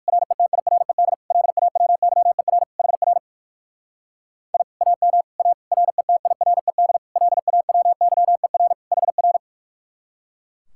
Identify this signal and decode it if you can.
Morse code, heard as a keyed tone, about 60 wpm, 700 Hz. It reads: RETIRED LAWYER HR I AM A RETIRED LAWYER HR